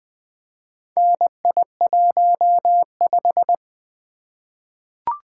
Morse code, heard as a keyed tone, about 21 wpm, 700 Hz. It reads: NI15
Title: Four Letter-Number Combinations - 20wpm